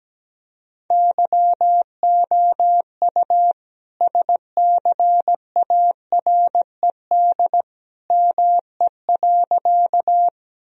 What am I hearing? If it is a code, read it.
Morse code, heard as a keyed tone, about 17 wpm, 700 Hz. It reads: YOU SCARED ME.
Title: Sentences from Top 1000 Words - Review Part 4 - 17wpm